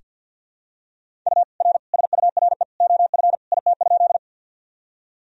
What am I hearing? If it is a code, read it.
Morse code, heard as a keyed tone, about 46 wpm, 700 Hz. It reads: U R SURE OF IT?
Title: Sentences from Top 300 Words and Abbreviations 50wpm